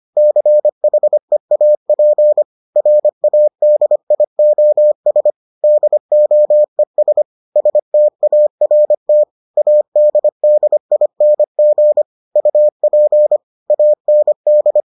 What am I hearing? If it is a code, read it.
Morse code, heard as a keyed tone, about 25 wpm, 600 Hz. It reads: CHEAP RADIOS DOES START ADDING UP AND